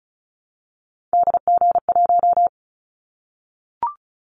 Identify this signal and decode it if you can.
Morse code, heard as a keyed tone, about 36 wpm, 700 Hz. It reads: DG1
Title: Callsign Prefixes for Top 50 DXCC - 35wpm